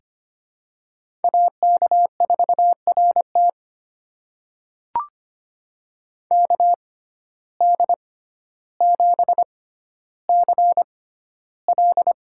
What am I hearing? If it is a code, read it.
Morse code, heard as a keyed tone, about 25 wpm, spaced as for 13 wpm, 700 Hz. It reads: AK4RT KD7CL
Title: USA Call Signs - 5 Characters - 6x Character Spacing - 25wpm